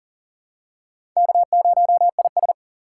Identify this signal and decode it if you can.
Morse code, heard as a keyed tone, about 40 wpm, 700 Hz. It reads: K0IS
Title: USA Call Signs - 4 Characters - 4x Character Spacing - 40wpm